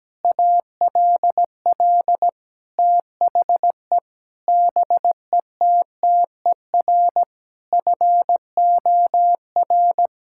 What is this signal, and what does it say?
Morse code, heard as a keyed tone, about 17 wpm, 700 Hz. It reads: ALL THE BETTER FOR
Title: Sentences from Top 1000 Words - Review Part 5 - 17wpm